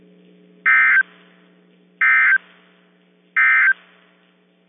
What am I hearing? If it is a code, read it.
Emergency Alert System (SAME) tones, heard as harsh buzzing bursts. It is heard from 0.5 to 3.8 s.
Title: KIH24 NOAA Weather Radio